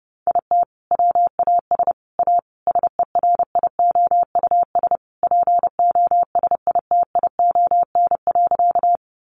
Morse code, heard as a keyed tone, about 30 wpm, 700 Hz. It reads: IT WAS A SERIOUS POSITION.
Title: Sentences from Top 1100 Words 30wpm